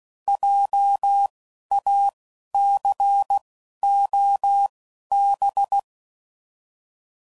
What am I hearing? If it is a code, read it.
Morse code, heard as a keyed tone, about 16 wpm, spaced as for 12 wpm, 800 Hz. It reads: JACOB